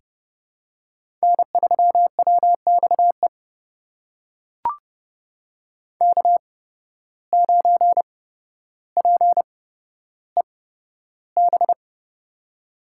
Morse code, heard as a keyed tone, about 30 wpm, spaced as for 13 wpm, 700 Hz. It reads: N3WXE K9PEB
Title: USA Call Signs - 5 Characters - 8x Character Spacing - 30wpm